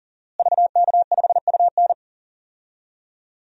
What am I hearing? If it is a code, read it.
Morse code, heard as a keyed tone, about 37 wpm, 700 Hz. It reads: VK5UN
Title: DXCC-100 Call Signs - 5 Characters - Repeated 3x - 40wpm